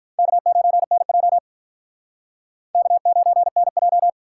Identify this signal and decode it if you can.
Morse code, heard as a keyed tone, about 43 wpm, 700 Hz. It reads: K9NJ K9NJ
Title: Speed Racing - USA 1x2 Call Signs 35wpm